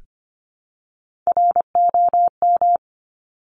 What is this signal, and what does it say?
Morse code, heard as a keyed tone, about 25 wpm, 700 Hz. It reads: ROM
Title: Top Three Letter Combinations in Words 25wpm